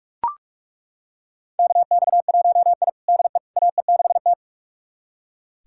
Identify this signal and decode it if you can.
Morse code, heard as a keyed tone, about 41 wpm, 700 Hz. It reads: KX1I DE AE6T
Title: USA Callsign DE Callsign - 4 Characters - 45wpm